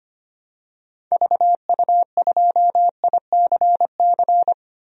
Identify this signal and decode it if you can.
Morse code, heard as a keyed tone, about 25 wpm, 700 Hz. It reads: VU2ICC
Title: DXCC-100 Call Signs - 6 Characters - 25wpm